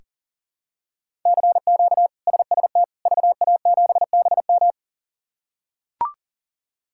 Morse code, heard as a keyed tone, about 40 wpm, 700 Hz. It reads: CQ SST VA7BM
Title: K1USN SST QSO Practice - 40wpm